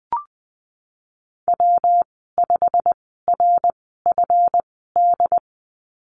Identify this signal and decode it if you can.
Morse code, heard as a keyed tone, about 20 wpm, spaced as for 15 wpm, 700 Hz. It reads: W5RFD
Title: 231 Lesson 37g - Letter x - Call Signs 20wpm